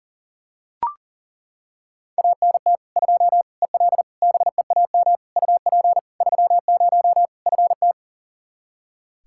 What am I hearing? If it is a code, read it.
Morse code, heard as a keyed tone, about 40 wpm, 700 Hz. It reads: ANT 2 EL BEAM UP 30 FT
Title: Practice QSO - Exchange 2 - Info - 40wpm